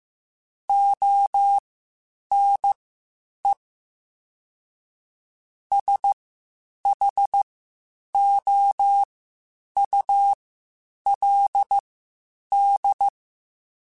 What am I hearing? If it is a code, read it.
Morse code, heard as a keyed tone, about 15 wpm, spaced as for 8 wpm, 800 Hz. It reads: ONE SHOULD
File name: Spaced / Chuckles Spaced-mp3s / BrainMouth sp @13